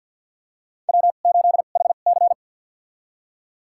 Morse code, heard as a keyed tone, about 50 wpm, spaced as for 36 wpm, 700 Hz. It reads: W8HC
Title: USA Call Signs - 2x Character Spacing - 50wpm